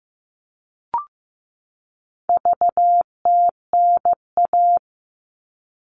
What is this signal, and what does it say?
Morse code, heard as a keyed tone, about 15 wpm, 700 Hz. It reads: VTNA
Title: All Cut Numbers - 4-Digit 15wpm